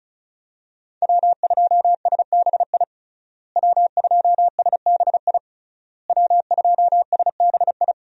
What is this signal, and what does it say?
Morse code, heard as a keyed tone, about 34 wpm, 700 Hz. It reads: W2SBI W2SBI W2SBI
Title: USA Call Signs - 5 Characters - Repeated 3x - 35wpm